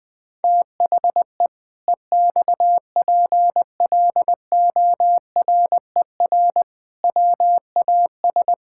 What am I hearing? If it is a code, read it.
Morse code, heard as a keyed tone, about 20 wpm, 700 Hz. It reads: THE EXPLORER WAS
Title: Pangram Challenge v2 - 20wpm